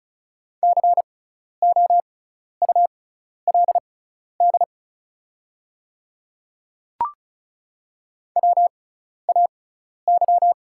Morse code, heard as a keyed tone, about 34 wpm, spaced as for 12 wpm, 700 Hz. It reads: COULD WAY
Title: 6x Character Spacing - Top 200 Words - 35wpm